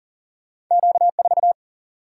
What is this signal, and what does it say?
Morse code, heard as a keyed tone, about 40 wpm, 700 Hz. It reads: Q 4
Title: Single-Letter Single-Number 40wpm